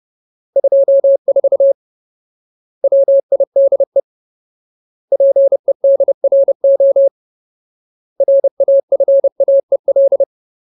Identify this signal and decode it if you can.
Morse code, heard as a keyed tone, about 30 wpm, 550 Hz. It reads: 24 WIDE PEDRO RAFAEL
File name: Ultimate Word List / 30WPM Ultimate